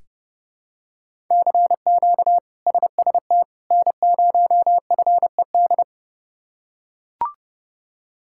Morse code, heard as a keyed tone, about 30 wpm, 700 Hz. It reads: CQ SST N0FED
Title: K1USN SST QSO Practice - 30wpm